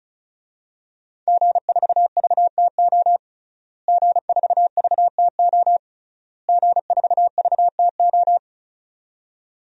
Morse code, heard as a keyed tone, about 34 wpm, 700 Hz. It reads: G4VTO G4VTO G4VTO
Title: DXCC-100 Call Signs - 5 Characters - Repeated 3x - 35wpm